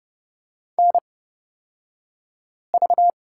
Morse code, heard as a keyed tone, about 31 wpm, spaced as for 5 wpm, 700 Hz. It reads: NV
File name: Lesson 36d - BT - Two Characters 30wpm